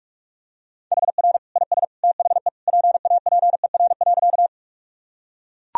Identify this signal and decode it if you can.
Morse code, heard as a keyed tone, about 41 wpm, 700 Hz. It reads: HR IS THE PAPER.
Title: Sentences from Top 300 Words and Abbreviations 45wpm